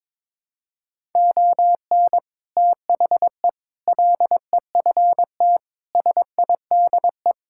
Morse code, heard as a keyed tone, about 22 wpm, 700 Hz. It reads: ON THE LEFT SIDE
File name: practice / 22wpm / Sets of 6 Words - 22wpm